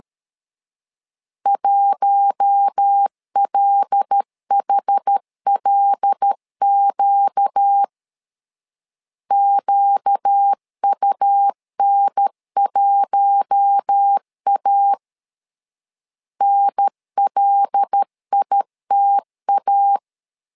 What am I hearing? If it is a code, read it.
Morse code, heard as a keyed tone, about 13 wpm, 800 Hz. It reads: 1LHLQ QUN1A NLITA
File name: LEZIONI / 5 LBVC / 5 L 12wpm 1913